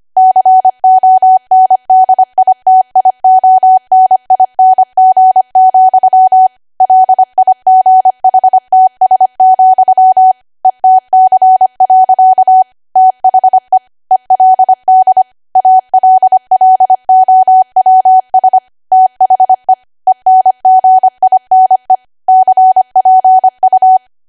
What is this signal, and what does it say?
Morse code, heard as a keyed tone, about 25 wpm, 750 Hz. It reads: CONDITIONING, LIGHTS, ETC. THE ELD ALLOWS THE ENGINE CPU
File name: Archive / 25 WPM / 25WPM